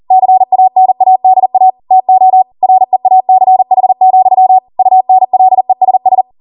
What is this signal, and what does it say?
Morse code, heard as a keyed tone, about 40 wpm, 750 Hz. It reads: CANADA TO REACH, UNLESS